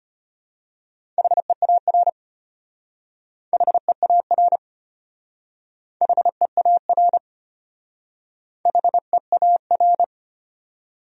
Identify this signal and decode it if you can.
Morse code, heard as a keyed tone, about 32 wpm, 700 Hz. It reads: HEAR HEAR HEAR HEAR